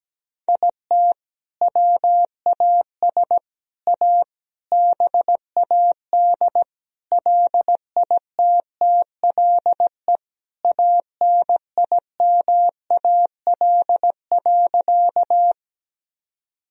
Morse code, heard as a keyed tone, about 17 wpm, 700 Hz. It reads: IT WAS A BAD LITTLE ANIMAL.
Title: Sentences from Top 1000 Words - Review Part 1 - 17wpm